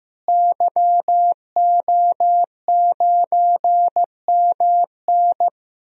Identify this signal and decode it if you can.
Morse code, heard as a keyed tone, about 15 wpm, 700 Hz. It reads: YO9MN
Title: DXCC-100 Call Signs - 5 Characters - Repeated 3x - 15wpm